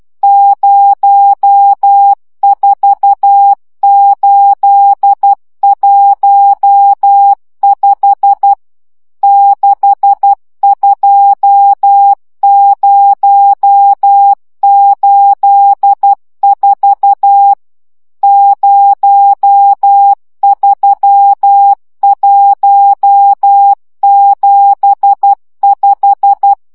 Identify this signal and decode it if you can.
Morse code, heard as a keyed tone, about 12 wpm, 800 Hz. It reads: 04815 62084 03175